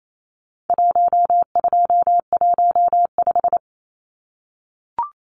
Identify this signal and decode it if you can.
Morse code, heard as a keyed tone, about 28 wpm, 700 Hz. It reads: 1215
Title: Numbers - 4-Digit 28wpm